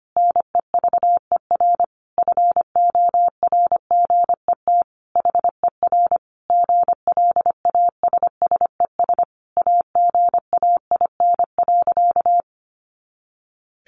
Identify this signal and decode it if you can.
Morse code, heard as a keyed tone, about 25 wpm, 700 Hz. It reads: NEVER FORGET HER GLASSES AGAIN.
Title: Additional Easy Phrases 25wpm